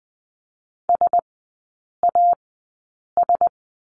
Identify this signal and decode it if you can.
Morse code, heard as a keyed tone, about 20 wpm, spaced as for 8 wpm, 700 Hz. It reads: SAS
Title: 035 Lesson 7e - Letter s - Three Characters 20wpm